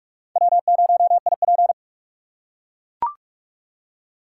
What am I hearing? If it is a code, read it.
Morse code, heard as a keyed tone, about 43 wpm, 700 Hz. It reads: W0IP